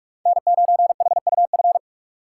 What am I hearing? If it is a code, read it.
Morse code, heard as a keyed tone, about 46 wpm, 700 Hz. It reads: N9HUF
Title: USA Call Signs - 5 Characters - 3x Character Spacing - 45wpm